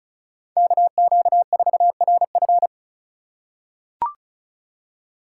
Morse code, heard as a keyed tone, about 36 wpm, 700 Hz. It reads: KQ4RF
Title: USA Call Signs - 4x Character Spacing - 35wpm